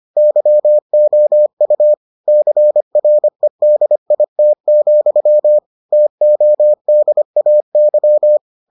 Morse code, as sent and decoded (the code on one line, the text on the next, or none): -.-- --- ..- / -.-. .-. -.. .. - --..-- / - --- -.. .- -.--
YOU CREDIT, TODAY